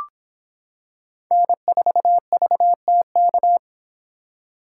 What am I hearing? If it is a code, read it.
Morse code, heard as a keyed tone, about 26 wpm, 700 Hz. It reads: N4VTK